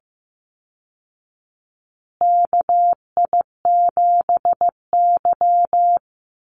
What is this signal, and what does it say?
Morse code, heard as a keyed tone, about 15 wpm, 700 Hz. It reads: KI7Y
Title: Call Signs Repeated 2x - 15wpm